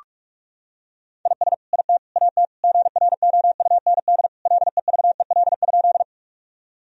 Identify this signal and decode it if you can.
Morse code, heard as a keyed tone, about 41 wpm, 700 Hz. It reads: IS IT AT GROUND LEVEL?
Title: Sentences from Top 1000 Words - Review Part 4 - 45wpm